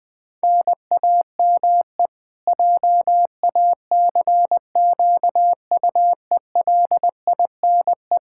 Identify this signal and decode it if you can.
Morse code, heard as a keyed tone, about 20 wpm, 700 Hz. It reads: NAME JACQUELINE
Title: Practice QSO - Exchange 1 - Beginning - 20wpm